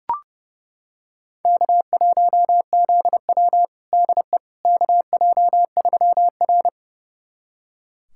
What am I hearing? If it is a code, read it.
Morse code, heard as a keyed tone, about 30 wpm, 700 Hz. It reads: K1ZW DE KJ3R
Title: USA Callsign DE Callsign - 4 Characters - 30wpm